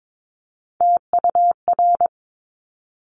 Morse code, heard as a keyed tone, about 22 wpm, 700 Hz. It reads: TUR
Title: Top Three Letter Combinations in Words 22wpm